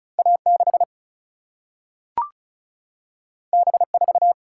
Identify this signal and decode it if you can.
Morse code, heard as a keyed tone, about 36 wpm, 700 Hz. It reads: A6 B4
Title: Single-Letter Single-Number 35wpm